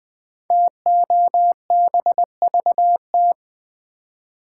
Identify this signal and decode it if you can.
Morse code, heard as a keyed tone, about 20 wpm, 700 Hz. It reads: TOBVT
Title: Code Groups - 5 Letters - 20wpm